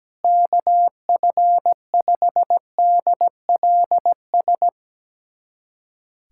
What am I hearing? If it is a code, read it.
Morse code, heard as a keyed tone, about 17 wpm, 700 Hz. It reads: KF5DLS
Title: USA Callsign DE Callsign - 6 Characters - 17wpm